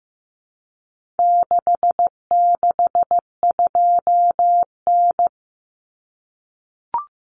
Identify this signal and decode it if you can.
Morse code, heard as a keyed tone, about 15 wpm, 700 Hz. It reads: 662N
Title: Common Cut Numbers - 4-Digit 15wpm